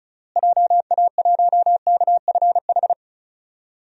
Morse code, heard as a keyed tone, about 36 wpm, 700 Hz. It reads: JA1KFH
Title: DXCC-100 Call Signs - 6 Characters - Repeated 3x - 35wpm